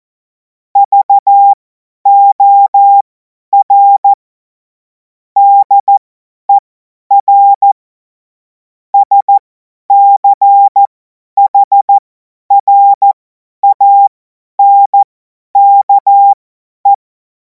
Morse code, heard as a keyed tone, about 14 wpm, spaced as for 10 wpm, 800 Hz. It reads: VOR DER SCHRANKE